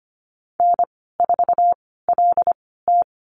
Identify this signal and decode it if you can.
Morse code, heard as a keyed tone, about 25 wpm, spaced as for 16 wpm, 690 Hz. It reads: N4LT
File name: Lesson 15g - Number 5 - Call Signs 25wpm